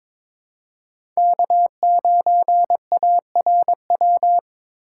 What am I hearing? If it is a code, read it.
Morse code, heard as a keyed tone, about 23 wpm, 700 Hz. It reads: K9ARW
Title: USA Call Signs - 5 Characters - 4x Character Spacing - 22wpm